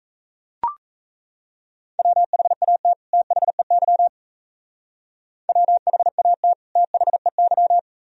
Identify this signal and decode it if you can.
Morse code, heard as a keyed tone, about 34 wpm, 700 Hz. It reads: WHAT THEY WHAT THEY
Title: Speed-Racing - Sets of 2 Words 28wpm